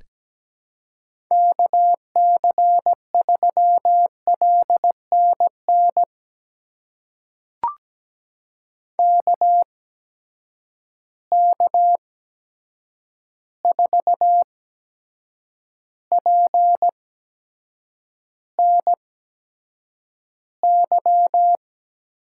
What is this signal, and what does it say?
Morse code, heard as a keyed tone, about 17 wpm, spaced as for 7 wpm, 700 Hz. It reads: KC3LNN KK4PNY
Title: USA Call Signs - 6 Characters - 8x Character Spacing - 17wpm